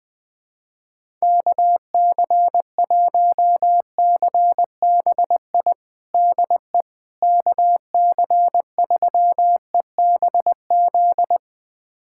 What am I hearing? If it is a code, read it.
Morse code, heard as a keyed tone, about 20 wpm, 700 Hz. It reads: KC1CBI DE KC3EBZ
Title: USA Callsign DE Callsign - 6 Characters - 20wpm